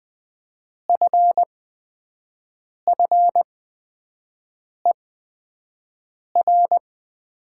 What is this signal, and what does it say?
Morse code, heard as a keyed tone, about 20 wpm, spaced as for 5 wpm, 700 Hz. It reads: FFER
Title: 8x Character Spacing - Top 300 Words - 20wpm